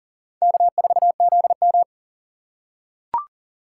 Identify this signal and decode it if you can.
Morse code, heard as a keyed tone, about 40 wpm, 700 Hz. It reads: K4ZM
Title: USA Call Signs - 2x Character Spacing - 40wpm